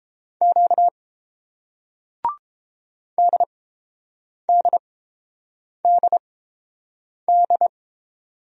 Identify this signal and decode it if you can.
Morse code, heard as a keyed tone, about 30 wpm, spaced as for 7 wpm, 700 Hz. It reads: Q DDDD